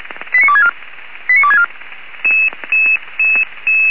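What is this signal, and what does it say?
Five-tone selective call (ZVEI), heard as quick runs of five tones, tones of about 70 ms. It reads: 89264 89264
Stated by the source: ZVEI five-tone selective call received on an SDR.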